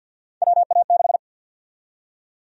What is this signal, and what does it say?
Morse code, heard as a keyed tone, about 51 wpm, 700 Hz. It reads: WA6
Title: USA Call Sign Prefixes - 50wpm